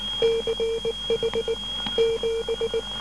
Morse code, heard as a keyed tone, about 19 wpm, 460 Hz, with noise. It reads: CH 7
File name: SAG6 12880.5 1218